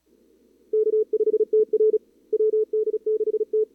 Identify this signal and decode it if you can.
Morse code, heard as a keyed tone, about 35 wpm, 440 Hz. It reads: K5TR WD6T
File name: CWSprint / Feb2022 / k5tr-feb-na-sprint-cw-k5ot -0000